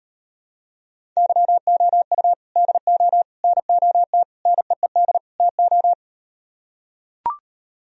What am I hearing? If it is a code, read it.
Morse code, heard as a keyed tone, about 38 wpm, 700 Hz. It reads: YOU DO NOT NEED TO